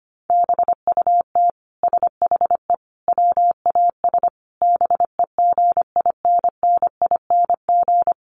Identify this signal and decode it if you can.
Morse code, heard as a keyed tone, about 25 wpm, 700 Hz. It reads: BUT SHE WAS BEGINNING
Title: Sentences from Top 1500 Words 25wpm